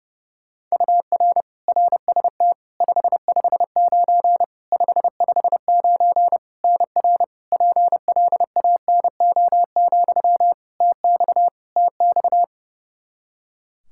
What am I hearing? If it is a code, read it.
Morse code, heard as a keyed tone, about 30 wpm, 700 Hz. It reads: UR RST 559 559 NR PLANO, TX TX
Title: Practice QSO - Exchange 1 - Beginning - 30wpm